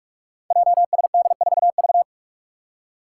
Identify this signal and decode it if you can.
Morse code, heard as a keyed tone, about 44 wpm, 700 Hz. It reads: JSD4V
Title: Five Challenging Letter-Number Combinations 45wpm